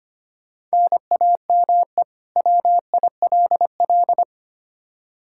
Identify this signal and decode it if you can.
Morse code, heard as a keyed tone, about 25 wpm, 700 Hz. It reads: NAME WILL
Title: K1USN SST QSO Practice 25wpm